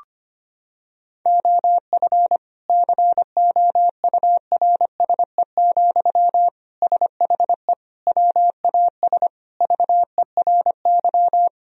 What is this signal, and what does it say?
Morse code, heard as a keyed tone, about 25 wpm, 700 Hz. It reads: OF COURSE, SHE WAS VERY
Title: Sentences from Top 1000 Words - Review Part 4 - 25wpm